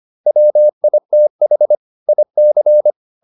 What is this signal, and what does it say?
Morse code, heard as a keyed tone, about 25 wpm, 600 Hz. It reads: WITH IC